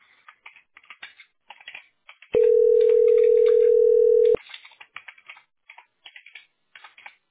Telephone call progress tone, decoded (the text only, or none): ringback tone